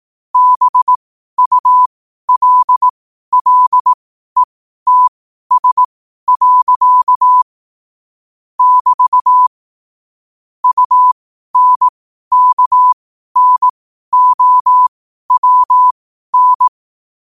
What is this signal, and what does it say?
Morse code, heard as a keyed tone, about 18 wpm, spaced as for 12 wpm, 1000 Hz. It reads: BULLETS. <BT> UNKNOWN
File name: qotdfiles / 12wpm